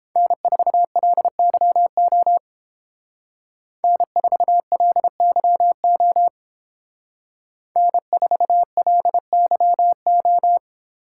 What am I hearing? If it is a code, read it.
Morse code, heard as a keyed tone, about 28 wpm, 700 Hz. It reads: N4LYO N4LYO N4LYO